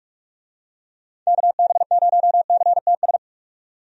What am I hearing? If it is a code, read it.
Morse code, heard as a keyed tone, about 46 wpm, 700 Hz. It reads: KB0CTS
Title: USA Call Signs - 6 Characters - 45wpm